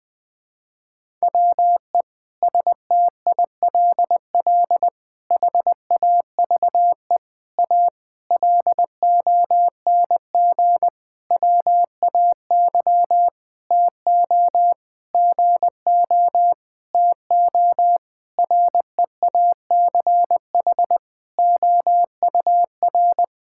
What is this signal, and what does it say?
Morse code, heard as a keyed tone, about 20 wpm, 700 Hz. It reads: WE STILL HAVE A LONG WAY TO GO TO REACH OUR
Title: Sentences with 7-Word Phrases - Spoken Beginning - 20wpm